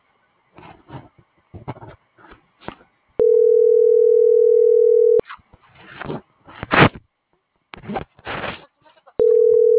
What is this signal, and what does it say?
Ringback tone (North America), 440 + 480 Hz.